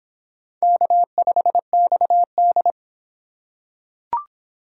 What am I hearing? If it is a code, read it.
Morse code, heard as a keyed tone, about 25 wpm, 700 Hz. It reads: K5XD